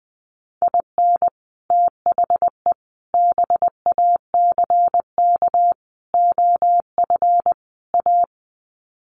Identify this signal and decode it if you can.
Morse code, heard as a keyed tone, about 20 wpm, 700 Hz. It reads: IN THE BACK OF A
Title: Sets of 5 Words Encore2 20wpm